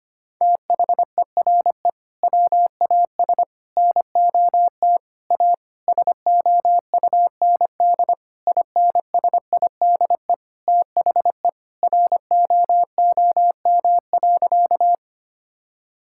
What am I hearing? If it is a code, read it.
Morse code, heard as a keyed tone, about 25 wpm, 700 Hz. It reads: THERE WAS NOT A SOUND INSIDE THE ROOM.
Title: Sentences from Top 1000 Words - Review Part 6 - 25wpm